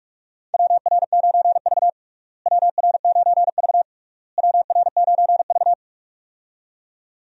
Morse code, heard as a keyed tone, about 46 wpm, 700 Hz. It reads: WR9V WR9V WR9V
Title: USA Call Signs - 4 Characters - Repeated 3x - 45wpm